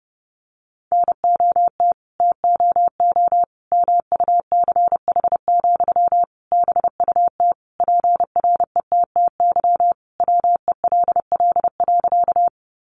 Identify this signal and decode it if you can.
Morse code, heard as a keyed tone, about 28 wpm, 700 Hz. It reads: NOT TOO MUCH, BUT PRETTY WELL.
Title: Sentences from Top 1000 Words 30wpm